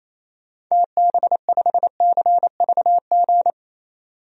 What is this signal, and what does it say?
Morse code, heard as a keyed tone, about 28 wpm, 700 Hz. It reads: TB5CVG
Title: DXCC-100 Call Signs - 6 Characters - 28wpm